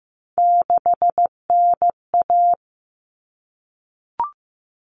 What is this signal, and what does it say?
Morse code, heard as a keyed tone, about 15 wpm, 700 Hz. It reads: 6NA
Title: Common Cut Numbers - 3-Digit 15wpm